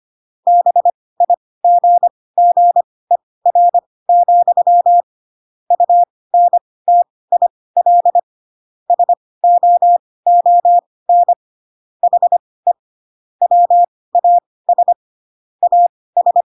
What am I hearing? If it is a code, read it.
Morse code, heard as a keyed tone, about 25 wpm, spaced as for 18 wpm, 700 Hz. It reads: BIGGER, UNTIL SOON HE WAS AS